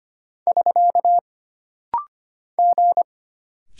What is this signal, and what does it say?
Morse code, heard as a keyed tone, about 25 wpm, 700 Hz. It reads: <SK> G
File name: Single Character (Letter:Number:Punctuation:Prosign) - Rapid-Fire - 25wpm